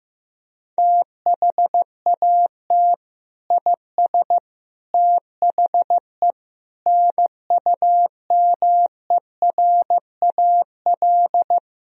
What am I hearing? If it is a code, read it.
Morse code, heard as a keyed tone, about 15 wpm, 700 Hz. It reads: THAT IS THE NUMERAL
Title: Sentences from Top 1000 Words - Review Part 2 - 15wpm